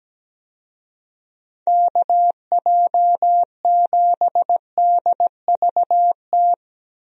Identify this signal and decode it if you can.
Morse code, heard as a keyed tone, about 17 wpm, 700 Hz. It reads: KJ7DVT
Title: USA Call Signs - 6 Characters - Repeated 3x - 17wpm